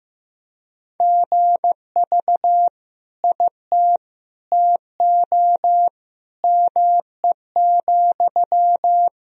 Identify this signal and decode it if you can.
Morse code, heard as a keyed tone, about 15 wpm, 700 Hz. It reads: GV IT TO ME,
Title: Sentences from Top 200 Words and Abbreviations 15wpm